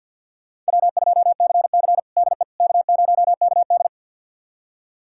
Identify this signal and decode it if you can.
Morse code, heard as a keyed tone, about 52 wpm, 700 Hz. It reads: W2CC DE K0KD
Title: USA Callsign DE Callsign - 4 Characters - 50wpm